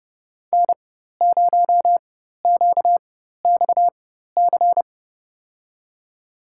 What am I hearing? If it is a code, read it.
Morse code, heard as a keyed tone, about 30 wpm, spaced as for 14 wpm, 700 Hz. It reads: N0QXC
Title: USA Call Signs - 5 Characters - 4x Character Spacing - 30wpm